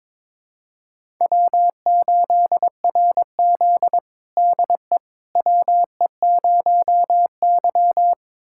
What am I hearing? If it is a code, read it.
Morse code, heard as a keyed tone, about 22 wpm, 700 Hz. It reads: W8RZ DE WE0Y
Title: USA Callsign DE Callsign - 4 Characters - 22wpm